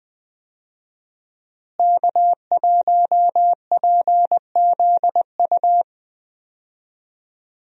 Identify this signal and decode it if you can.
Morse code, heard as a keyed tone, about 20 wpm, 700 Hz. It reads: K1PZU
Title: USA Call Signs - 5 Characters - Repeated 2x - 20wpm